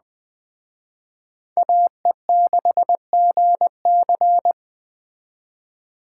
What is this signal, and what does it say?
Morse code, heard as a keyed tone, about 20 wpm, 700 Hz. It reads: AE6GC
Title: Speed Racing - USA Call Signs 17wpm